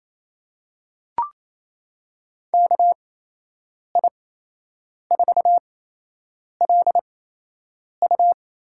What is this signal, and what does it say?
Morse code, heard as a keyed tone, about 28 wpm, spaced as for 8 wpm, 700 Hz. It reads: KI4LU
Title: USA Call Signs - 8x Character Spacing - 28wpm